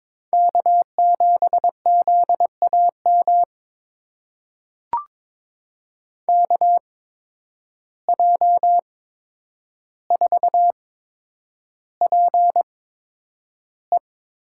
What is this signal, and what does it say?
Morse code, heard as a keyed tone, about 22 wpm, spaced as for 9 wpm, 700 Hz. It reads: K7ZAM KJ4PE
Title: USA Call Signs - 5 Characters - 8x Character Spacing - 22wpm